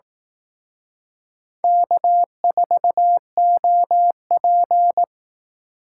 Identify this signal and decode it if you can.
Morse code, heard as a keyed tone, about 18 wpm, 700 Hz. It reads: K4OP
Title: Speed Racing - USA Call Signs 15wpm